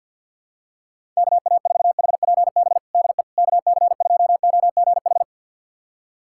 Morse code, heard as a keyed tone, about 47 wpm, 700 Hz. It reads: KA4HPB DE KC2OGH
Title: USA Callsign DE Callsign 50wpm